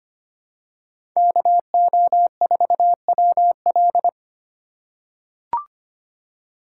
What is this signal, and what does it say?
Morse code, heard as a keyed tone, about 25 wpm, 700 Hz. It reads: KO4WL
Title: USA Call Signs - 5 Characters - 3x Character Spacing - 25wpm